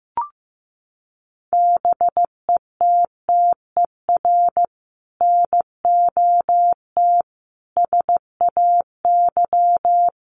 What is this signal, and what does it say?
Morse code, heard as a keyed tone, about 15 wpm, 700 Hz. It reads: BETTER NOT SAY
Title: Sentences from Top 1000 Words 15wpm